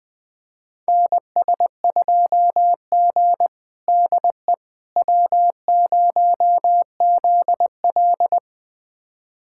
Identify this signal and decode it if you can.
Morse code, heard as a keyed tone, about 20 wpm, 700 Hz. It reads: NS2G DE W0ZL
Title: USA Callsign DE Callsign - 4 Characters - 20wpm